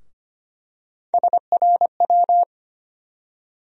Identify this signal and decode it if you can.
Morse code, heard as a keyed tone, about 25 wpm, 700 Hz. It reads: SRW
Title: USA Call Sign Postfixes - 3 Letters 25wpm